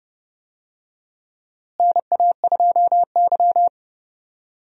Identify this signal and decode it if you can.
Morse code, heard as a keyed tone, about 29 wpm, 700 Hz. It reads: NA2Y
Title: USA Call Signs - 4 Characters - Repeated 3x - 30wpm